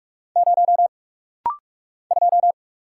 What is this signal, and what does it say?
Morse code, heard as a keyed tone, about 44 wpm, 700 Hz. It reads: 0 2
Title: Single Letter-Number - Rapid-Fire 45wpm